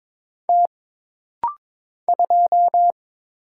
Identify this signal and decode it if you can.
Morse code, heard as a keyed tone, about 22 wpm, 700 Hz. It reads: T 2
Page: This rapid-fire timing is geared to push your ICR (Instant Character Recognition) skills.